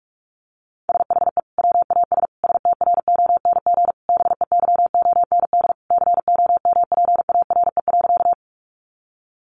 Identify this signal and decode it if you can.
Morse code, heard as a keyed tone, about 44 wpm, 700 Hz. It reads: SHE WAS STRONG BEYOND COMPARE.
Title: Sentences from Top 1300 Words 45wpm